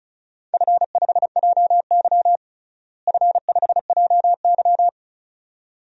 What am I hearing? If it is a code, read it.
Morse code, heard as a keyed tone, about 36 wpm, 700 Hz. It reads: F5JY F5JY
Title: DXCC-100 Call Signs - 4 Characters - Repeated 2x - 35wpm